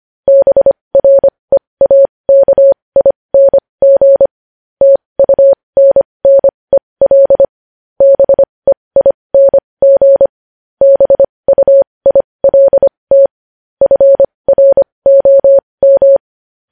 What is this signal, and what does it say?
Morse code, heard as a keyed tone, about 25 wpm, spaced as for 20 wpm, 550 Hz. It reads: BREAKING TUNNEL BEING BUILT FROM
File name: podcast / News headlines-1x-25x20